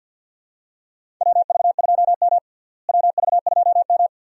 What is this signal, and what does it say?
Morse code, heard as a keyed tone, about 49 wpm, 700 Hz. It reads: WV2M WV2M
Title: USA Call Signs - 4 Characters - Repeated 2x - 50wpm